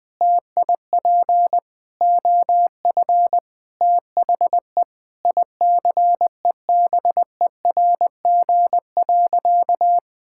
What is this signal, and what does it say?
Morse code, heard as a keyed tone, about 20 wpm, 700 Hz. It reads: TIP OF THE ICEBERG.
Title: Sentences with 6-Word Phrases - Spoken Beginning - 20wpm